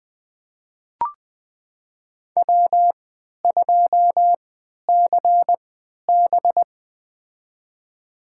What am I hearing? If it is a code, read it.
Morse code, heard as a keyed tone, about 21 wpm, spaced as for 11 wpm, 700 Hz. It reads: W2CB
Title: USA Call Signs - 3x Character Spacing - 20wpm